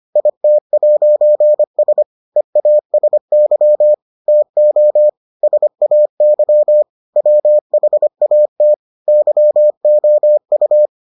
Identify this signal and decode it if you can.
Morse code, heard as a keyed tone, about 25 wpm, 600 Hz. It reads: IT'S EASY TO SAY WHAT YOU